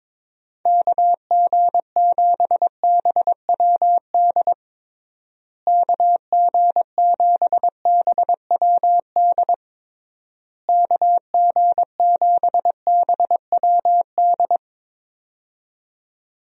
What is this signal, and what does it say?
Morse code, heard as a keyed tone, about 22 wpm, 700 Hz. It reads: KG7BWD KG7BWD KG7BWD
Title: USA Call Signs - 6 Characters - Repeated 3x - 22wpm